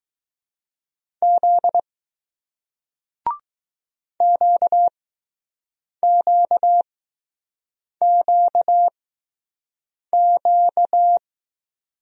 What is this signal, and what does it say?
Morse code, heard as a keyed tone, about 20 wpm, spaced as for 6 wpm, 700 Hz. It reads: Z QQQQ